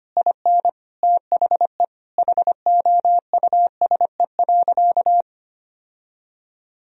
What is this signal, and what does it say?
Morse code, heard as a keyed tone, about 25 wpm, 690 Hz. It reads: IN THE HOUSE.
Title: Sentences from Top 1000 Words - Review Part 4 - 25wpm